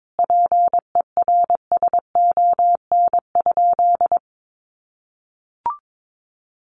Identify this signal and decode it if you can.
Morse code, heard as a keyed tone, about 22 wpm, 700 Hz. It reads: PERSON?
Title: Sentences from Top 400 Words 22wpm